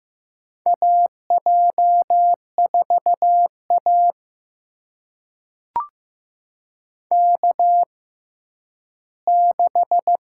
Morse code, heard as a keyed tone, about 15 wpm, spaced as for 7 wpm, 700 Hz. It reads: AJ4AK 6
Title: USA Call Signs - 6x Character Spacing - 15wpm